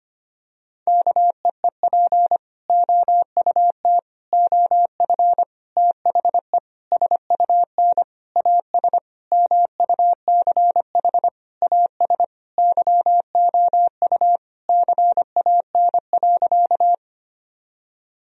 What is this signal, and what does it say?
Morse code, heard as a keyed tone, about 25 wpm, 700 Hz. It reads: KEEP OUT OF THE SUN AS MUCH AS YOU CAN.
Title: Sentences from Top 1000 Words - Review Part 1 - 25wpm